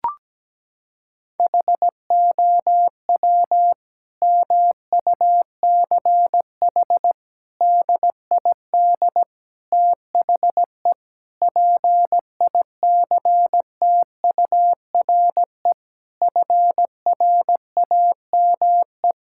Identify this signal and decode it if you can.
Morse code, heard as a keyed tone, about 17 wpm, 700 Hz. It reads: HOW MUCH DID THE PICTURE FRAME
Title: Sentences from Top 1000 Words - Review Part 6 - 17wpm